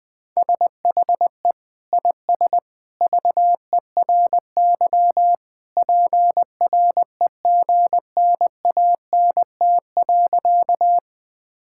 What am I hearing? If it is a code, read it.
Morse code, heard as a keyed tone, about 20 wpm, 700 Hz. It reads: SHE IS VERY PREGNANT.
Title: Sentences from Top 1000 Words - Review Part 6 - 20wpm